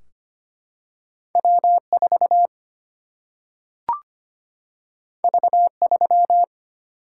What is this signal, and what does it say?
Morse code, heard as a keyed tone, about 25 wpm, 700 Hz. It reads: W 4 V 3
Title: Single-Letter Single-Number 25wpm